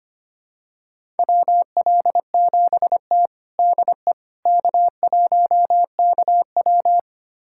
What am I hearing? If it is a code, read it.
Morse code, heard as a keyed tone, about 25 wpm, 700 Hz. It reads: WL7T DE K1KW
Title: USA Callsign DE Callsign - 4 Characters - 25wpm